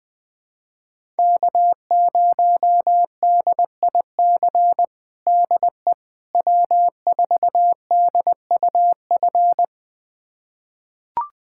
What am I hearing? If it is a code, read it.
Morse code, heard as a keyed tone, about 21 wpm, 700 Hz. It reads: K0DIC DE W4DUF
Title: USA Callsign DE Callsign - 5 Characters - 20wpm